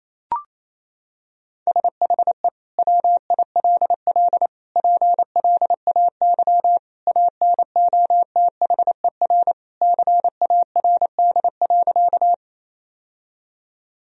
Morse code, heard as a keyed tone, about 28 wpm, 700 Hz. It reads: SHE WILL PLAY ANOTHER CARD.
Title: Sentences from Top 1000 Words - Review Part 4 - 28wpm